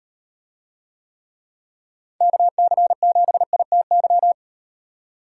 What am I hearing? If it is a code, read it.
Morse code, heard as a keyed tone, about 38 wpm, 700 Hz. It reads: KC7ITY